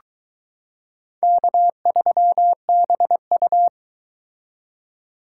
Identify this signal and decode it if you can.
Morse code, heard as a keyed tone, about 22 wpm, 700 Hz. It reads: K3BU